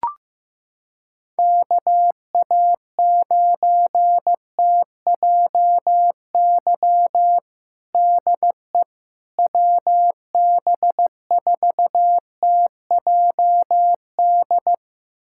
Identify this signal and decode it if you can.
Morse code, heard as a keyed tone, about 15 wpm, 700 Hz. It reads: KA9TJY DE WB4TJD
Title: USA Callsign DE Callsign 15wpm